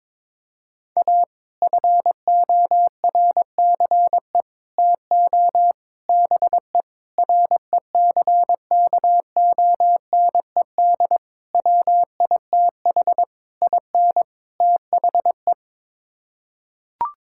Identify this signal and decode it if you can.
Morse code, heard as a keyed tone, about 22 wpm, 700 Hz. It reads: A FORCE TO BE RECKONED WITH IN THE
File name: practice / 22wpm / Sets of 8 Words - 22wpm